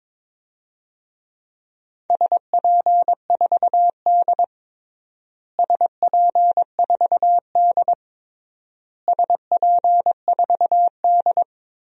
Morse code, heard as a keyed tone, about 22 wpm, 700 Hz. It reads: SP4D SP4D SP4D
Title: DXCC-100 Call Signs - 4 Characters - Repeated 3x - 22wpm